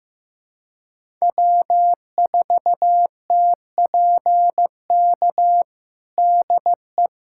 Morse code, heard as a keyed tone, about 15 wpm, 700 Hz. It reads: W4TPK DE
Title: USA Callsign DE Callsign - 5 Characters - 15wpm